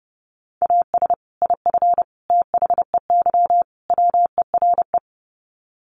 Morse code, heard as a keyed tone, about 30 wpm, 700 Hz. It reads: AS IF THEY WERE
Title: Sets of 4 Words 30wpm